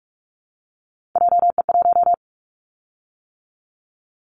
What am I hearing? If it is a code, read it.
Morse code, heard as a keyed tone, about 44 wpm, 700 Hz. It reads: JE1
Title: Callsign Prefixes for Top 25 DXCC - 45wpm